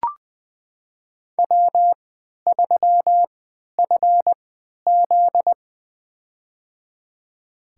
Morse code, heard as a keyed tone, about 20 wpm, spaced as for 11 wpm, 700 Hz. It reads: W3FZ
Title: USA Call Signs - 3x Character Spacing - 20wpm